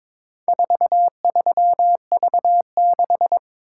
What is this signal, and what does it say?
Morse code, heard as a keyed tone, about 23 wpm, 700 Hz. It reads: 43V6
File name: practice / 22wpm / Four Letter-Number Combinations - 22wpm